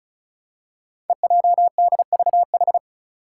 Morse code, heard as a keyed tone, about 36 wpm, 700 Hz. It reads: EJDVH
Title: Five Challenging Letter-Number Combinations 35wpm